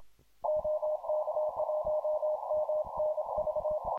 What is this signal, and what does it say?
Morse code, heard as a keyed tone, about 26 wpm, 610 Hz, with noise. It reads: OK1MNV